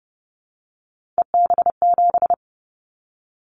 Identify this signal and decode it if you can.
Morse code, heard as a keyed tone, about 30 wpm, 700 Hz. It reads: EB7
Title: Callsign Prefixes for Top 25 DXCC - 30wpm